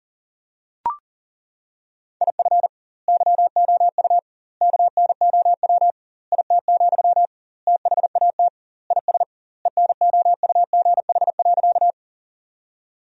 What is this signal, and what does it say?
Morse code, heard as a keyed tone, about 37 wpm, 700 Hz. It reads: IF YOU KNOW IT, THAT IS ENOUGH.
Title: 2x Word-Spacing - Sentences from Top 400 Words - 40wpm